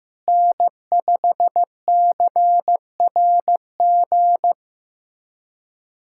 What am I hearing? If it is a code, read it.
Morse code, heard as a keyed tone, about 15 wpm, 700 Hz. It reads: N5CRG
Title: USA Callsign DE Callsign - 5 Characters - 15wpm